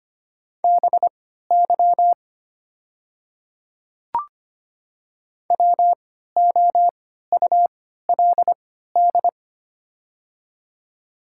Morse code, heard as a keyed tone, about 25 wpm, spaced as for 14 wpm, 700 Hz. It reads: BY WOULD
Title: 3x Character Spacing - Top 100 Words - 25wpm